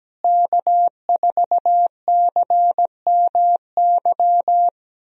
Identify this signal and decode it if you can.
Morse code, heard as a keyed tone, about 17 wpm, 700 Hz. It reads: K4CMY
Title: USA Call Signs - 5 Characters - Repeated 2x - 17wpm